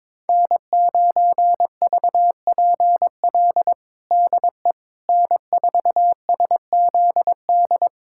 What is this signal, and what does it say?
Morse code, heard as a keyed tone, about 22 wpm, 700 Hz. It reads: N9VPL DE N4SZD
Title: USA Callsign DE Callsign - 5 Characters - 22wpm